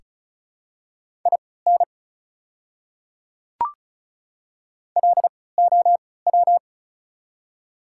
Morse code, heard as a keyed tone, about 34 wpm, spaced as for 20 wpm, 690 Hz. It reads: IN LOW
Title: 3x Character Spacing - Top 200 Words - 35wpm